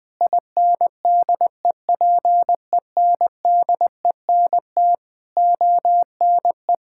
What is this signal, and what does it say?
Morse code, heard as a keyed tone, about 20 wpm, 700 Hz. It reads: INDEPENDENT ONE
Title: Sentences from Top 1000 Words - Review Part 6 - 20wpm